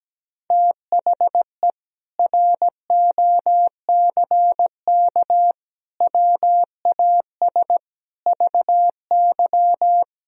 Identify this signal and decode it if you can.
Morse code, heard as a keyed tone, about 17 wpm, 700 Hz. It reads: THE ROCK WAS VY
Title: Sentences from Top 400 Words and Abbreviations 17wpm